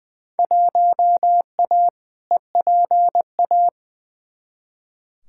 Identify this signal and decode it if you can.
Morse code, heard as a keyed tone, about 20 wpm, 700 Hz. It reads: 1A EPA
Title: Field Day Exchanges - 20wpm